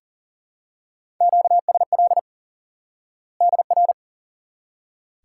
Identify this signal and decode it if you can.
Morse code, heard as a keyed tone, about 40 wpm, 700 Hz. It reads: QSL DR
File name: IQE Course 3 - Lesson 89 - YAGI - Two Elements - 40wpm